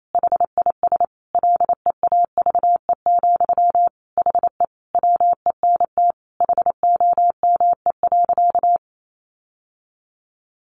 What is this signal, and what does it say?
Morse code, heard as a keyed tone, about 28 wpm, 700 Hz. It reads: HIS LEAVE, HE WENT HOME.
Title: Sentences from Top 1200 Words 28wpm